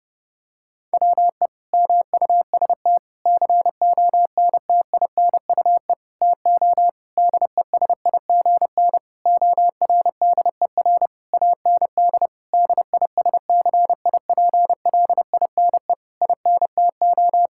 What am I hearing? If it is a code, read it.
Morse code, heard as a keyed tone, about 31 wpm, 700 Hz. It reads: WE MUST CONTINUE TO DESIGN ORDER AND DISCIPLINE INTO
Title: Sentences from Top 1000 Words - Review Part 6 - 30wpm